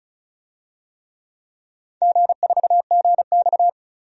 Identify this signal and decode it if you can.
Morse code, heard as a keyed tone, about 36 wpm, 700 Hz. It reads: G4GX
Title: DXCC-100 Call Signs - 4 Characters - 35wpm